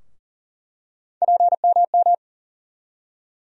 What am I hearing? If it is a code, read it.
Morse code, heard as a keyed tone, about 40 wpm, 700 Hz. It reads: PMM
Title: USA Call Sign Postfixes - 3 Letters 40wpm